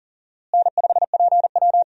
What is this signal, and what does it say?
Morse code, heard as a keyed tone, about 40 wpm, 700 Hz. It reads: N5PW
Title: USA Call Signs - 4 Characters - Repeated 2x - 40wpm